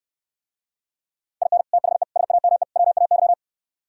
Morse code, heard as a keyed tone, about 67 wpm, 700 Hz. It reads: IN THE SAME WAY